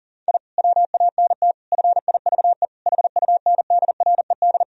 Morse code, heard as a keyed tone, about 40 wpm, 700 Hz. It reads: I WANT FIVE HUNDRED
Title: Sentences from Top 1000 Words - Review Part 2 - 40wpm